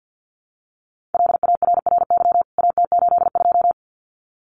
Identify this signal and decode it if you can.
Morse code, heard as a keyed tone, about 50 wpm, 700 Hz. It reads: LARRY AT82
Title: CWT Exchange 50wpm